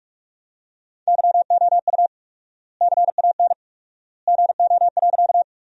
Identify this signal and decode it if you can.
Morse code, heard as a keyed tone, about 46 wpm, 700 Hz. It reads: YOU CAN GO.
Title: 4x Word-Spacing - Sentences from Top 100 Words - 45wpm